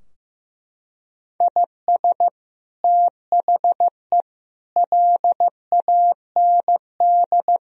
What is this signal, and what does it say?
Morse code, heard as a keyed tone, about 15 wpm, 700 Hz. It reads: IS THE LAND
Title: Sentences from Top 1000 Words - Review Part 2 - 15wpm